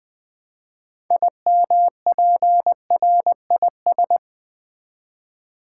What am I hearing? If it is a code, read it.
Morse code, heard as a keyed tone, about 20 wpm, 700 Hz. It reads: IMPRIS
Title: Build A Word - 2 Letters At A Time - 8-Letter Words - 20wpm